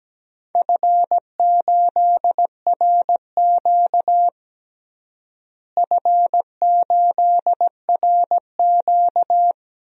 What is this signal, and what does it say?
Morse code, heard as a keyed tone, about 17 wpm, 700 Hz. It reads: F8RQ F8RQ
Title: DXCC-100 Call Signs - 4 Characters - Repeated 3x - 17wpm